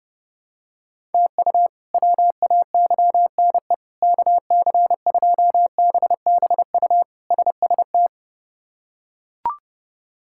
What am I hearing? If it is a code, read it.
Morse code, heard as a keyed tone, about 30 wpm, 700 Hz. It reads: TU WAYNE KC2BBU SST
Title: K1USN SST QSO Practice - 30wpm